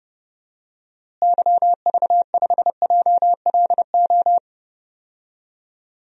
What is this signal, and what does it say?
Morse code, heard as a keyed tone, about 32 wpm, 700 Hz. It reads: YV5JLO
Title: DXCC-100 Call Signs - 6 Characters - Repeated 3x - 30wpm